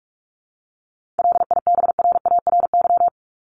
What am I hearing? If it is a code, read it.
Morse code, heard as a keyed tone, about 44 wpm, 700 Hz. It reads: LIBRARY
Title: Top 501-750 Words 45wpm